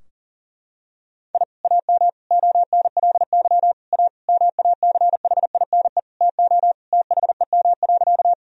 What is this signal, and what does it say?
Morse code, heard as a keyed tone, about 38 wpm, 700 Hz. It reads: I AM ONLY A MACHINE TO THEM.
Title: Sentences from Top 1000 Words - Review Part 2 - 40wpm